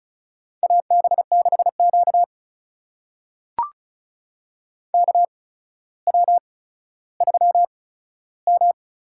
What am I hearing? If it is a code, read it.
Morse code, heard as a keyed tone, about 36 wpm, spaced as for 15 wpm, 700 Hz. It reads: AB6Q KW3M
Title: USA Call Signs - 4 Characters - 8x Character Spacing - 35wpm